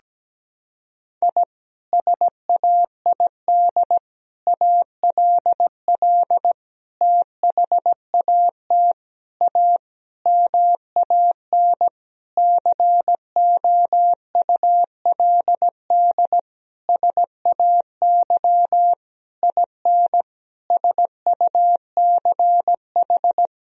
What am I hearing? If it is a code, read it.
Morse code, heard as a keyed tone, about 17 wpm, 700 Hz. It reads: I SAID ALL THAT A MAN COULD SAY IN SUCH
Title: Sentences from Top 1000 Words - Review Part 2 - 17wpm